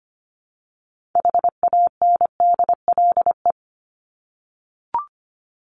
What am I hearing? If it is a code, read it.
Morse code, heard as a keyed tone, about 25 wpm, 700 Hz. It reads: HANDLE
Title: Top 1001-1250 Words 25wpm